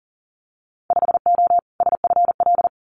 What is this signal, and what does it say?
Morse code, heard as a keyed tone, about 40 wpm, 700 Hz. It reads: 5O SFL
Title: Winter Field Day Exchanges 40wpm